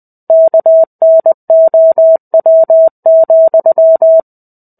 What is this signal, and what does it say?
Morse code, heard as a keyed tone, about 20 wpm, 650 Hz. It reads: KNOW,
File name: 05-Chapter-05 20 WPM